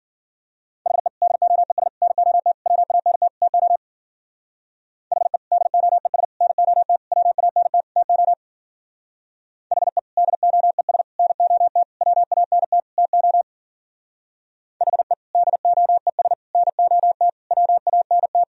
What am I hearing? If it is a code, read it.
Morse code, heard as a keyed tone, about 51 wpm, 700 Hz. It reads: HE DOES NOT WANT TO HE DOES NOT WANT TO HE DOES NOT WANT TO HE DOES NOT WANT